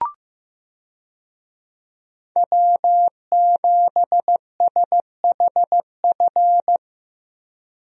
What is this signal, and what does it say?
Morse code, heard as a keyed tone, about 15 wpm, 700 Hz. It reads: W7SHF
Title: USA Call Signs - 5 Characters - Repeated 2x - 15wpm